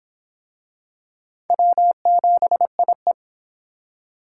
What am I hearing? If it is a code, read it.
Morse code, heard as a keyed tone, about 25 wpm, 700 Hz. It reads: W7IE